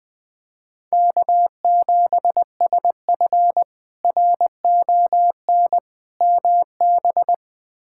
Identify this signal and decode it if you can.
Morse code, heard as a keyed tone, about 20 wpm, 700 Hz. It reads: K7SF RON MB
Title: K1USN SST QSO Practice - 20wpm